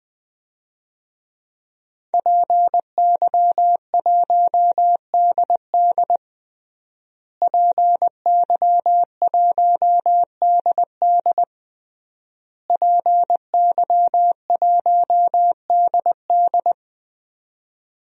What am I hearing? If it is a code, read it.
Morse code, heard as a keyed tone, about 20 wpm, 700 Hz. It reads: PY1DD PY1DD PY1DD
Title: DXCC-100 Call Signs - 5 Characters - Repeated 3x - 20wpm